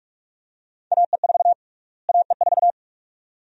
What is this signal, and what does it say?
Morse code, heard as a keyed tone, about 46 wpm, 700 Hz. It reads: AE4 AE4
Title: USA Call Sign Prefixes - Repeated 2x - 45wpm